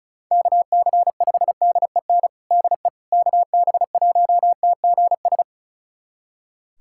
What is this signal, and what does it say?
Morse code, heard as a keyed tone, about 34 wpm, 700 Hz. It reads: KC5DEN DE KB1TGS
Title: USA Callsign DE Callsign - 6 Characters - 35wpm